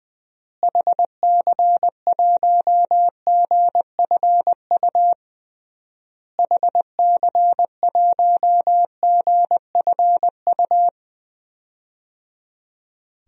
Morse code, heard as a keyed tone, about 20 wpm, 700 Hz. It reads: HC1GFU HC1GFU
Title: DXCC-100 Call Signs - 6 Characters - Repeated 3x - 20wpm